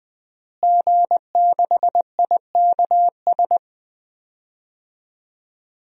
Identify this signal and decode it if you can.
Morse code, heard as a keyed tone, about 21 wpm, 700 Hz. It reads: G6IKS
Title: DXCC-100 Call Signs - 5 Characters - Repeated 2x - 20wpm